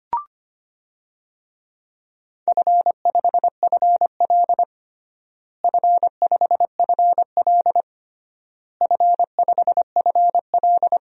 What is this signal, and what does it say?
Morse code, heard as a keyed tone, about 25 wpm, 700 Hz. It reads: F5FL F5FL F5FL
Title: DXCC-100 Call Signs - 4 Characters - Repeated 3x - 25wpm